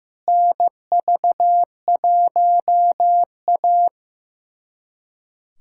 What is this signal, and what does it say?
Morse code, heard as a keyed tone, about 15 wpm, 700 Hz. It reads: NV1A
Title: USA Callsign DE Callsign - 4 Characters - 15wpm